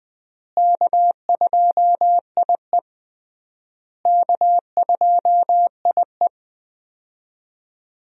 Morse code, heard as a keyed tone, about 20 wpm, 700 Hz. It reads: K2IE K2IE
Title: USA Call Signs - 4 Characters - Repeated 3x - 20wpm